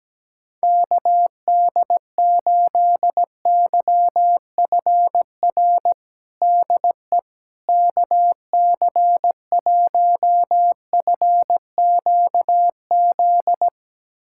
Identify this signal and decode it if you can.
Morse code, heard as a keyed tone, about 17 wpm, 700 Hz. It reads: KD8YFR DE KC1FQZ
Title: USA Callsign DE Callsign 17wpm